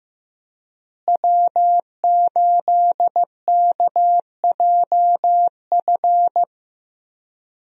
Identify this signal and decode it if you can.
Morse code, heard as a keyed tone, about 15 wpm, 700 Hz. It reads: W8KJF
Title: USA Call Signs - 5 Characters - Repeated 3x - 15wpm